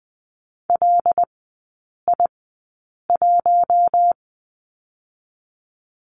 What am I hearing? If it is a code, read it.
Morse code, heard as a keyed tone, about 20 wpm, spaced as for 8 wpm, 700 Hz. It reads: LI1
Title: 087 Lesson 15e - Number 5 - Three Characters 20wpm